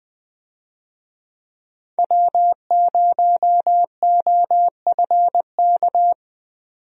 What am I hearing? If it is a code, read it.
Morse code, heard as a keyed tone, about 21 wpm, 700 Hz. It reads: W0OFK
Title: USA Call Signs - 5 Characters - 20wpm